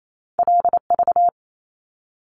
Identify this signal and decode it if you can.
Morse code, heard as a keyed tone, about 28 wpm, 700 Hz. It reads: LV
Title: Top Two Letter Combinations in Words 28wpm